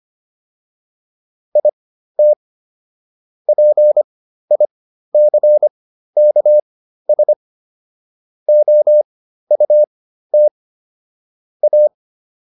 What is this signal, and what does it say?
Morse code, heard as a keyed tone, about 25 wpm, spaced as for 13 wpm, 600 Hz. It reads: IT PICKS OUT A